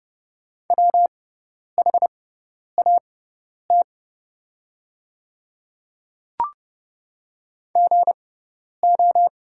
Morse code, heard as a keyed tone, about 30 wpm, spaced as for 10 wpm, 700 Hz. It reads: WHAT GO
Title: 6x Character Spacing - Top 100 Words - 30wpm